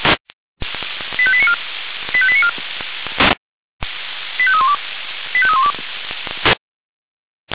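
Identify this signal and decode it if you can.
Five-tone selective call (ZVEI), heard as quick runs of five tones, tones of about 70 ms. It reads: 85894 85894 85312 85312
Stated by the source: ZVEI five-tone selective call for 85894, received by SDR.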